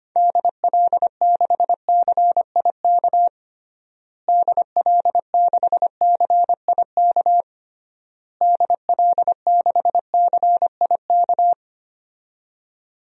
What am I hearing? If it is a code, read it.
Morse code, heard as a keyed tone, about 25 wpm, 700 Hz. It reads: DL6CIK DL6CIK DL6CIK
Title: DXCC-100 Call Signs - 6 Characters - Repeated 3x - 25wpm